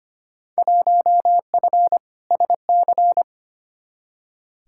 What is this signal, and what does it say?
Morse code, heard as a keyed tone, about 25 wpm, 700 Hz. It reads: 1F SC
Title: Field Day Exchanges Encore - 25wpm